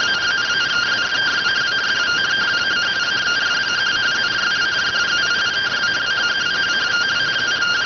Two-tone teleprinter signal, about 170 Hz shift.